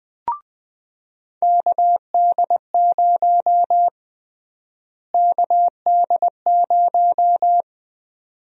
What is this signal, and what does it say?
Morse code, heard as a keyed tone, about 20 wpm, 700 Hz. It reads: KD0 KD0
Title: USA Call Sign Prefixes - Repeated 2x - 20wpm